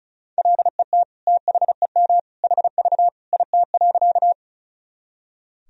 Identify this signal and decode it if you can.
Morse code, heard as a keyed tone, about 34 wpm, 700 Hz. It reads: LET THEM HV IT.
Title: Sentences from Top 300 Words and Abbreviations 35wpm